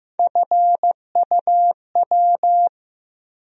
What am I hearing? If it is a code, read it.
Morse code, heard as a keyed tone, about 15 wpm, 700 Hz. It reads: FUW
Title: USA Call Sign Postfixes - 3 Letters 15wpm